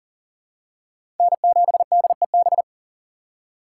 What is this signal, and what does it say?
Morse code, heard as a keyed tone, about 40 wpm, 700 Hz. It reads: N7DEB
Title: USA Call Signs - 5 Characters - 3x Character Spacing - 40wpm